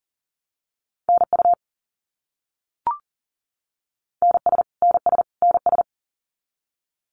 Morse code, heard as a keyed tone, about 40 wpm, 700 Hz. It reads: NU NS NS NS
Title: Canadian Provinces and Territories - Repeated 3x - 40wpm